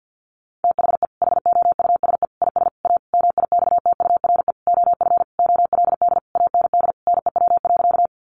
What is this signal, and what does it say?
Morse code, heard as a keyed tone, about 50 wpm, 700 Hz. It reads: THE HOUSE IS A MIXTURE OF OLD AND NEW.
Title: Sentences from Top 1500 Words 50wpm